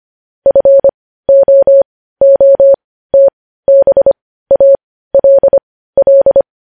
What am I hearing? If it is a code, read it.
Morse code, heard as a keyed tone, about 25 wpm, spaced as for 15 wpm, 550 Hz. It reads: FOOTBALL
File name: podcast / News headlines-1x-25x15